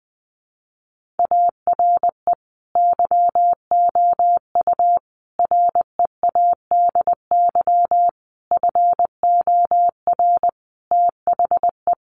Morse code, heard as a keyed tone, about 20 wpm, 700 Hz. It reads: ARE YOU READY FOR THE
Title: Sentences from Top 1500 Words 20wpm